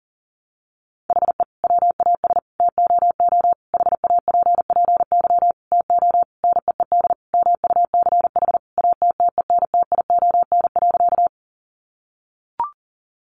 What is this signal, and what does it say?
Morse code, heard as a keyed tone, about 40 wpm, 700 Hz. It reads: HE WAS TOO HAPPY TO NEED MUCH ATTENTION.
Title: Sentences from Top 1000 Words 40wpm